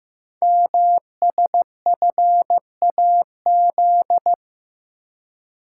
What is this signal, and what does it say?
Morse code, heard as a keyed tone, about 15 wpm, 700 Hz. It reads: MSFAZ
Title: Code Groups - 5 Letters - 15wpm